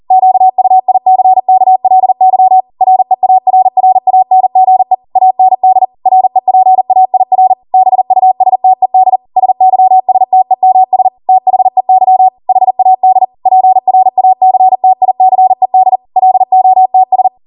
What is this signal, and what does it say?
Morse code, heard as a keyed tone, about 40 wpm, 750 Hz. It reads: QUICKLY REARRANGE AND REPAIR BUSTED SYSTEMS THEY HAD PRACTICED LOTS